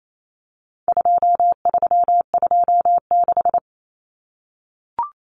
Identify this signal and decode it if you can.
Morse code, heard as a keyed tone, about 28 wpm, 700 Hz. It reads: 2326
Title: Common Cut Numbers - 4-Digit 28wpm